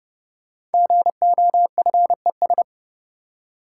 Morse code, heard as a keyed tone, about 30 wpm, 700 Hz. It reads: GOFES
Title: Code Groups - 5 Letters - 30wpm